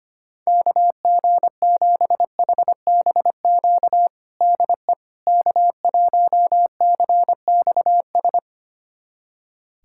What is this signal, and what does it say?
Morse code, heard as a keyed tone, about 25 wpm, 700 Hz. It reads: KG7HBQ DE K1CXS
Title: USA Callsign DE Callsign 25wpm